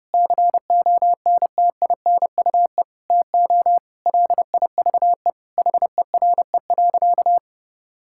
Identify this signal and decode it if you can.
Morse code, heard as a keyed tone, about 30 wpm, 700 Hz. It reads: CONTINUE TO LIVE HERE.
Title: Sentences from Top 1000 Words - Review Part 6 - 30wpm